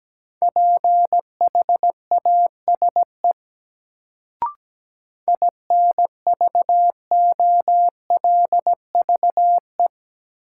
Morse code, heard as a keyed tone, about 17 wpm, 700 Hz. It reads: PHASE INVOLVE
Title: Sentences from Top 1200 Words - New Word Drill Alt - 17wpm